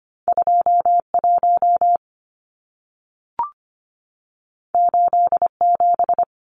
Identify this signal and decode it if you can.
Morse code, heard as a keyed tone, about 25 wpm, 700 Hz. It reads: 21 87
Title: Numbers - 2-Digit 25wpm